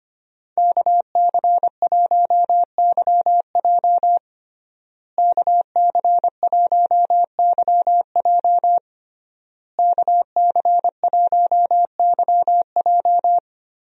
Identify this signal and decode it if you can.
Morse code, heard as a keyed tone, about 25 wpm, 700 Hz. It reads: KC1YJ KC1YJ KC1YJ
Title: USA Call Signs - 5 Characters - Repeated 3x - 25wpm